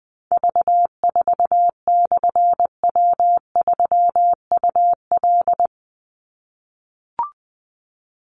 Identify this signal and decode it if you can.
Morse code, heard as a keyed tone, about 20 wpm, 700 Hz. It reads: V4/W3UL
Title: Call Signs - 20wpm